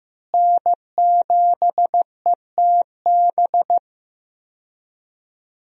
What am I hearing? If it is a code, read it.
Morse code, heard as a keyed tone, about 16 wpm, 700 Hz. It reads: N7ETB
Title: USA Call Signs - 5 Characters - Repeated 2x - 15wpm